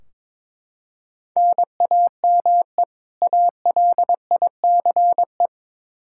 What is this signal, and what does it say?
Morse code, heard as a keyed tone, about 22 wpm, 700 Hz. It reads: NAME ALICE
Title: Practice QSO - Exchange 1 - Beginning - 22wpm